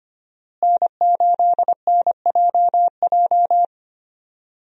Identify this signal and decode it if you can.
Morse code, heard as a keyed tone, about 25 wpm, 700 Hz. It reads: N8NJJ
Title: USA Call Signs - 5 Characters - Repeated 2x - 25wpm